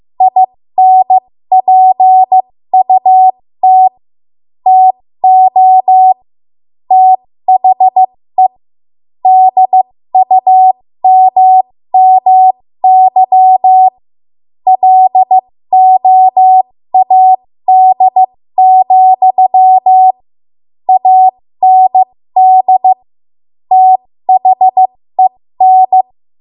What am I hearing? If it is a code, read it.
Morse code, heard as a keyed tone, about 15 wpm, spaced as for 13 wpm, 750 Hz. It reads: INPUT TO THE DUMMY LOAD, AND THEN
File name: Morse / 13WPM